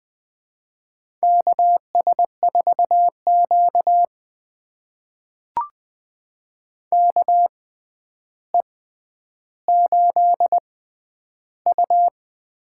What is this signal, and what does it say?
Morse code, heard as a keyed tone, about 20 wpm, spaced as for 10 wpm, 700 Hz. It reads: KS4Q KE8U
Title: USA Call Signs - 6x Character Spacing - 20wpm